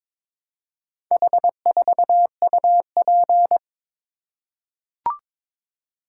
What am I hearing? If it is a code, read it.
Morse code, heard as a keyed tone, about 21 wpm, 700 Hz. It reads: H4UP
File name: practice / 22wpm / Four Letter-Number Combinations - 22wpm